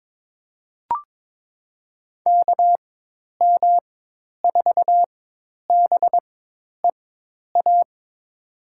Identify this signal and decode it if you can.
Morse code, heard as a keyed tone, about 21 wpm, spaced as for 10 wpm, 700 Hz. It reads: KM4BEA
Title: USA Call Signs - 6 Characters - 4x Character Spacing - 22wpm